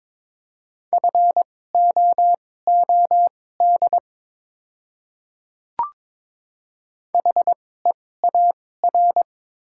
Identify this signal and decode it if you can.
Morse code, heard as a keyed tone, about 22 wpm, spaced as for 16 wpm, 700 Hz. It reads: FOOD HEAR
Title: 2x Character Spacing - Top 500 Words - 22wpm